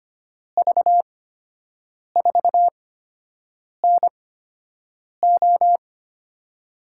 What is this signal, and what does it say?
Morse code, heard as a keyed tone, about 25 wpm, spaced as for 7 wpm, 700 Hz. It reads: V4NO
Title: USA Call Signs - 5 Characters - 8x Character Spacing - 25wpm